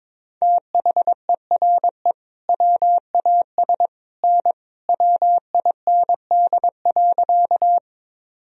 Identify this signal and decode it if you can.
Morse code, heard as a keyed tone, about 22 wpm, 700 Hz. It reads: THERE WAS N WIND.
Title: Sentences from Top 400 Words and Abbreviations 22wpm